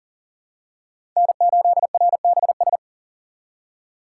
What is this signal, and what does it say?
Morse code, heard as a keyed tone, about 40 wpm, 690 Hz. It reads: N8RBS
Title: USA Call Signs - 4x Character Spacing - 40wpm